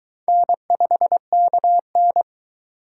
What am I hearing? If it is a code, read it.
Morse code, heard as a keyed tone, about 22 wpm, 700 Hz. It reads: N5KN